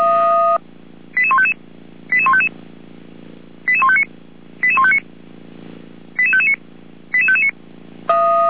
Five-tone selective call (ZVEI), heard as quick runs of five tones, tones of about 70 ms. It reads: 88155 88155 88159 88159 88559 88559